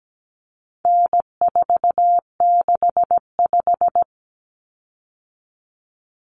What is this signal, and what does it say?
Morse code, heard as a keyed tone, about 17 wpm, 700 Hz. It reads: N465
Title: Common Cut Numbers - 4-Digit 17wpm